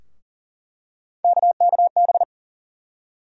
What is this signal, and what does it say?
Morse code, heard as a keyed tone, about 40 wpm, 700 Hz. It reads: KKB
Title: USA Call Sign Postfixes - 3 Letters 40wpm